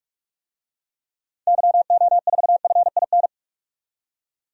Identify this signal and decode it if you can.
Morse code, heard as a keyed tone, about 46 wpm, 700 Hz. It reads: YO4FIN
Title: DXCC-100 Call Signs - 6 Characters - 45wpm